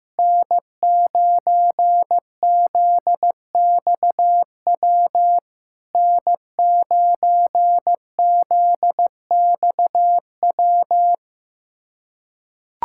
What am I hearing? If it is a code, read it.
Morse code, heard as a keyed tone, about 15 wpm, 690 Hz. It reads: N9ZXW N9ZXW
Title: SOTA QSO Practice 15wpm